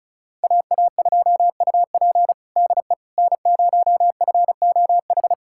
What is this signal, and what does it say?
Morse code, heard as a keyed tone, about 36 wpm, 700 Hz. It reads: AA2UP DE N0FOH
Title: USA Callsign DE Callsign - 5 Characters - 35wpm